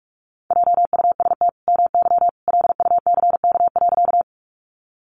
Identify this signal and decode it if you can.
Morse code, heard as a keyed tone, about 44 wpm, 700 Hz. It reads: JUST MY LUCK.
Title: Sentences from Top 1500 Words 45wpm